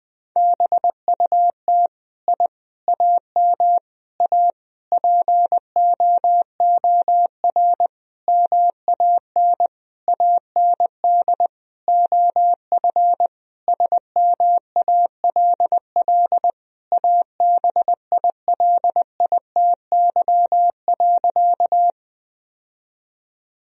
Morse code, heard as a keyed tone, about 20 wpm, 700 Hz. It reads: BUT I AM A POOR MAN AND OF SMALL ABILITY.
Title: Sentences from Top 1000 Words - Review Part 3 - 20wpm